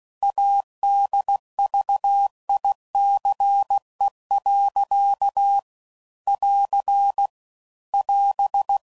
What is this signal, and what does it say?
Morse code, heard as a keyed tone, about 16 wpm, 800 Hz. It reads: ADVICE. <AR> <AS>